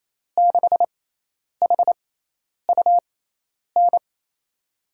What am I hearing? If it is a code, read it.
Morse code, heard as a keyed tone, about 28 wpm, spaced as for 10 wpm, 700 Hz. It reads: 6HUN
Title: USA Call Signs - 6 Characters - 6x Character Spacing - 28wpm